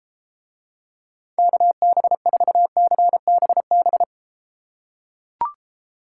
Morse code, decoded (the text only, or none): KB4CBB